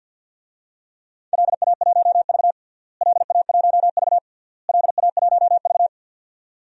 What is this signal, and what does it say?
Morse code, heard as a keyed tone, about 52 wpm, 700 Hz. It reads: LA1V LA1V LA1V
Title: DXCC-100 Call Signs - 4 Characters - Repeated 3x - 50wpm